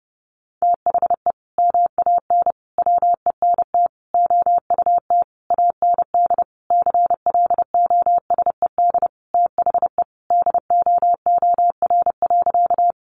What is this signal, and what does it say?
Morse code, heard as a keyed tone, about 30 wpm, 700 Hz. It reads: THE MAN WENT OUT AND CLOSED THE DOOR.
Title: Sentences from Top 1400 Words 30wpm